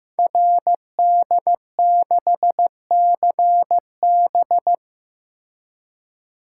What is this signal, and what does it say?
Morse code, heard as a keyed tone, about 15 wpm, 700 Hz. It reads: RD6CB
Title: Code Groups - 5 Letter:Numbers - 15wpm